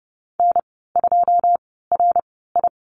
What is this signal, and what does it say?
Morse code, heard as a keyed tone, about 30 wpm, spaced as for 17 wpm, 690 Hz. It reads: N2RI
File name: Lesson 15g - Number 5 - Call Signs 30wpm